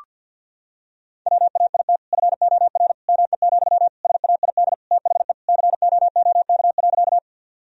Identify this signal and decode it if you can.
Morse code, heard as a keyed tone, about 50 wpm, 700 Hz. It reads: WAIT FOR ME, SAID THE COOK.
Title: Sentences from Top 1000 Words - Review Part 4 - 50wpm